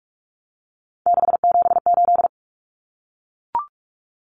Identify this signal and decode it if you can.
Morse code, heard as a keyed tone, about 44 wpm, 700 Hz. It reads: 678
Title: Common Cut Numbers - 3-Digit 45wpm